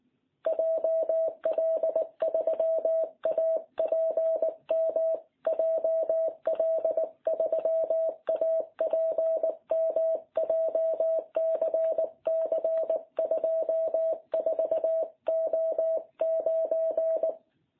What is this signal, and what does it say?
Morse code, heard as a keyed tone, about 19 wpm, 660 Hz. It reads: JL3APM JL3APM JCC 24O9